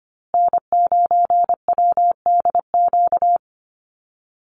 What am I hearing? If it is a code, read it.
Morse code, heard as a keyed tone, about 25 wpm, 700 Hz. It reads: N9WDQ
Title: USA Call Signs Repeated 2x 25wpm